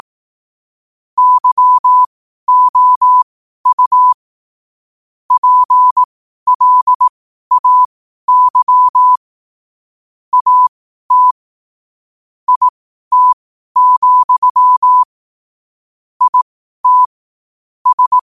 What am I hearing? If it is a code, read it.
Morse code, heard as a keyed tone, about 18 wpm, spaced as for 12 wpm, 1000 Hz. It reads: YOU PLAY AT IT, IT S
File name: qotdfiles / 12wpm